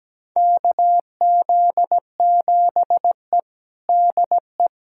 Morse code, heard as a keyed tone, about 17 wpm, 700 Hz. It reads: KZ7E DE
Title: USA Callsign DE Callsign - 4 Characters - 17wpm